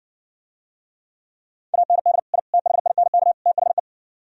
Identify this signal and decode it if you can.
Morse code, heard as a keyed tone, about 59 wpm, 700 Hz. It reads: AND I THINK THE